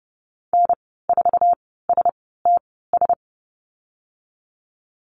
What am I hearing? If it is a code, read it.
Morse code, heard as a keyed tone, about 30 wpm, spaced as for 17 wpm, 700 Hz. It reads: N4STS
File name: Lesson 14g - Number 2 - Call Signs 30wpm